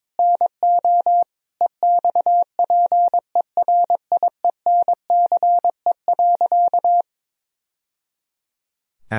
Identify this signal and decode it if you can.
Morse code, heard as a keyed tone, about 22 wpm, 690 Hz. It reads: NO EXPERIENCE.
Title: Sentences from Top 1000 Words - Review Part 3 - 22wpm